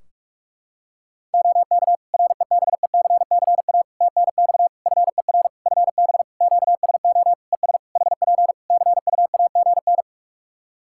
Morse code, heard as a keyed tone, about 46 wpm, 700 Hz. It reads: OK REBECCA TNX FER FB QSO ES HP CUAGN
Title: Practice QSO - Exchange 4 - Ending - 45wpm